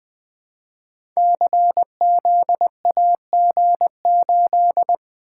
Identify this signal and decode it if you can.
Morse code, heard as a keyed tone, about 20 wpm, 700 Hz. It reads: CZAG8
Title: Code Groups - 5 Letter:Numbers - 20wpm